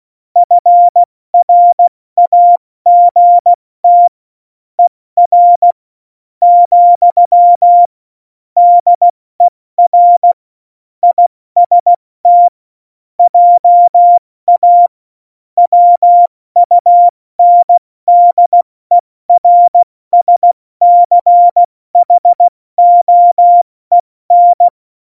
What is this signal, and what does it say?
Morse code, heard as a keyed tone, about 16 wpm, spaced as for 14 wpm, 700 Hz. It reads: FRAGT ER, DER IST JA WUNDERSCHOEN
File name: Morsekurs / Witze / Witz 04. September 2018, 70 F80